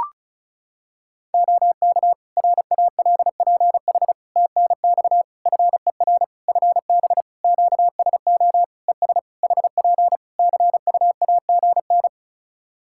Morse code, heard as a keyed tone, about 34 wpm, 700 Hz. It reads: OK RALPH TNX FER FB QSO ES HP CUAGN
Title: Practice QSO - Exchange 4 - Ending - 35wpm